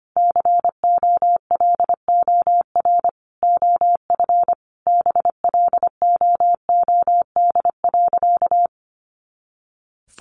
Morse code, heard as a keyed tone, about 25 wpm, 700 Hz. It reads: COLOR OF BLOOD.
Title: A Sound Of Thunder 25wpm